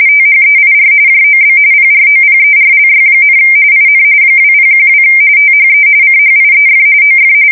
RTTY, about 45.45 baud. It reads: THEIR JOB PROPERLY, IN 2012 THE CHIEF OF
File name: fsk170 / usnavy5